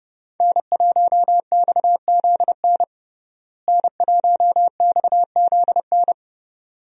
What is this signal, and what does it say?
Morse code, heard as a keyed tone, about 32 wpm, 700 Hz. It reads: N1XZN N1XZN
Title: USA Call Signs - 5 Characters - Repeated 2x - 30wpm